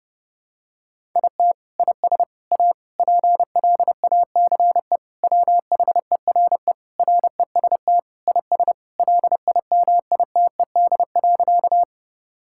Morse code, heard as a keyed tone, about 28 wpm, 700 Hz. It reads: IT IS A PLACE WHERE REST IS LIMITED.
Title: Sentences from Top 1000 Words - Review Part 6 - 30wpm